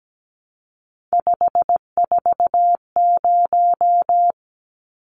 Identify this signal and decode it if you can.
Morse code, heard as a keyed tone, about 17 wpm, 700 Hz. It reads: 540
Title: Numbers - 3-Digit 17wpm